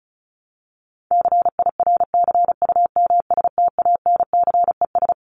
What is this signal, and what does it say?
Morse code, heard as a keyed tone, about 34 wpm, 700 Hz. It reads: CIRCUMSTANCES
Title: ES:S Suffix - Top 100 Words with Base Words 35wpm